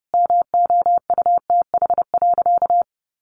Morse code, heard as a keyed tone, about 30 wpm, 700 Hz. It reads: MOUTH.
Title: Sentences from Top 1500 Words 30wpm